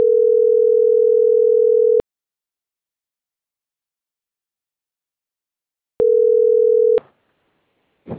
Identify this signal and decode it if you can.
Ringback tone (North America), 440 + 480 Hz.